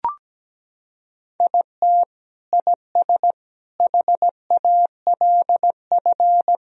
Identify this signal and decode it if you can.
Morse code, heard as a keyed tone, about 17 wpm, 700 Hz. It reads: IT IS HALF
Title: Sentences from Top 1000 Words - Review Part 1 - 17wpm